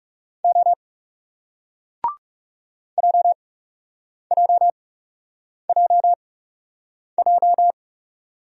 Morse code, heard as a keyed tone, about 39 wpm, spaced as for 8 wpm, 700 Hz. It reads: O JJJJ